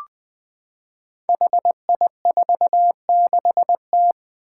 Morse code, heard as a keyed tone, about 20 wpm, 700 Hz. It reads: HI46T
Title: Five Challenging Letter-Number Combinations 20wpm